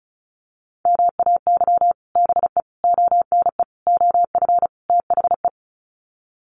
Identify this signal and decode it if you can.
Morse code, heard as a keyed tone, about 34 wpm, 700 Hz. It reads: MAY BE ONE OF THE
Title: Sets of 5 Words Encore2 35wpm